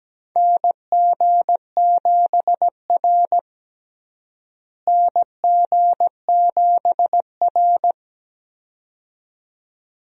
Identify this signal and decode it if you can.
Morse code, heard as a keyed tone, about 17 wpm, 700 Hz. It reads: NG7R NG7R
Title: USA Call Signs - 4 Characters - Repeated 2x - 17wpm